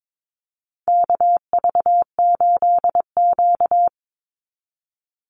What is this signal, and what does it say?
Morse code, heard as a keyed tone, about 22 wpm, 700 Hz. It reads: KV8Q
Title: Call Signs - 22wpm